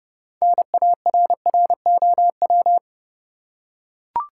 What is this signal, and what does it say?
Morse code, heard as a keyed tone, about 31 wpm, 700 Hz. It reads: NARROW
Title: Six Letter Words - 30wpm